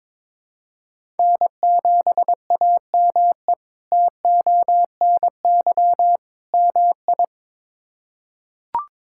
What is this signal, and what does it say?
Morse code, heard as a keyed tone, about 22 wpm, 700 Hz. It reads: N7AME TONY MI
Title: K1USN SST QSO Practice - 22wpm